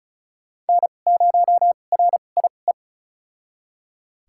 Morse code, heard as a keyed tone, about 34 wpm, spaced as for 25 wpm, 700 Hz. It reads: N0RIE
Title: USA Call Signs - 5 Characters - 2x Character Spacing - 35wpm